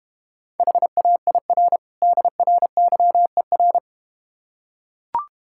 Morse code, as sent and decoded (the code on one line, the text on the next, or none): .... .- .. .-. / -.. .-. -.-- . .-.
HAIR DRYER